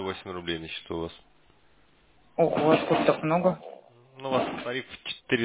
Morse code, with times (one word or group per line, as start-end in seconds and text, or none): none